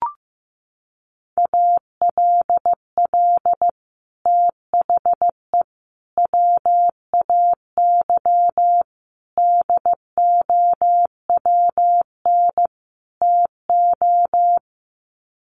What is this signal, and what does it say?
Morse code, heard as a keyed tone, about 15 wpm, 700 Hz. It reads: ALL THE WAY DOWN TO
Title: Sets of 5 Words Encore2 15wpm